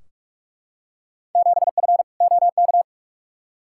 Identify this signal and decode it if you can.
Morse code, heard as a keyed tone, about 43 wpm, 700 Hz. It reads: 7F OK
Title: Field Day Exchanges Encore - 45wpm